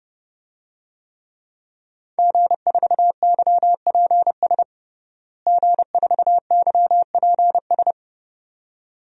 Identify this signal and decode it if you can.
Morse code, heard as a keyed tone, about 32 wpm, 700 Hz. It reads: G4YPS G4YPS
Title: DXCC-100 Call Signs - 5 Characters - Repeated 2x - 30wpm